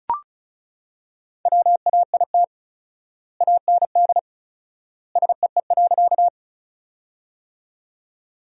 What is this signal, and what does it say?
Morse code, heard as a keyed tone, about 36 wpm, 700 Hz. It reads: WAIT AND SEE.
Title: 4x Word-Spacing - Sentences from Top 500 Words - 35wpm